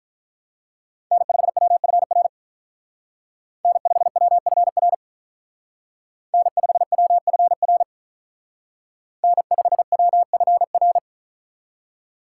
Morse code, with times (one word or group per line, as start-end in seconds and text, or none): 1.1-2.3 N5WFR
3.6-4.9 N5WFR
6.3-7.8 N5WFR
9.2-11.0 N5WFR